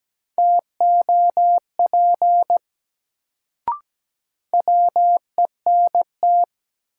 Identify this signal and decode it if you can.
Morse code, heard as a keyed tone, about 17 wpm, 700 Hz. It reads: TOP WENT
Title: Sets of 2 Words - New Word Drill - 17wpm